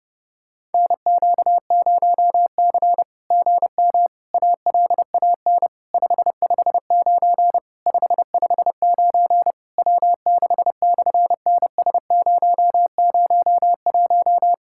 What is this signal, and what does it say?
Morse code, heard as a keyed tone, about 30 wpm, 700 Hz. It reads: NQ0C GM ALAN 559 559 W6/NS001
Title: SOTA QSO Practice 30wpm